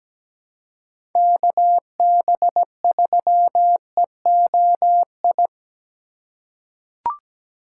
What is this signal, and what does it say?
Morse code, heard as a keyed tone, about 17 wpm, 700 Hz. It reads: KB3EOI
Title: USA Call Signs - 6 Characters - 3x Character Spacing - 17wpm